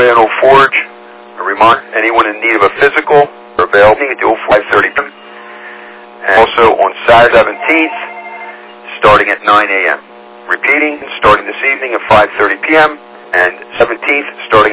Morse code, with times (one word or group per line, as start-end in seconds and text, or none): none